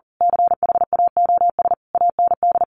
Morse code, heard as a keyed tone, about 40 wpm, 700 Hz. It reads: CHAOS AND
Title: A Sound Of Thunder 40wpm